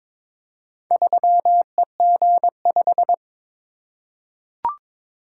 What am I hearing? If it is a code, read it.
Morse code, heard as a keyed tone, about 22 wpm, 700 Hz. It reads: 3EG5
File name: practice / 22wpm / Four Letter-Number Combinations - 22wpm